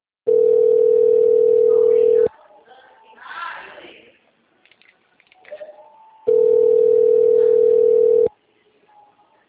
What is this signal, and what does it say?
Ringback tone (North America), 440 + 480 Hz, with speech.
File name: out- -2005- - - .28973